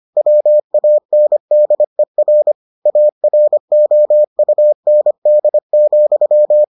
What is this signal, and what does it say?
Morse code, heard as a keyed tone, about 25 wpm, 600 Hz. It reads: WANDER AROUND,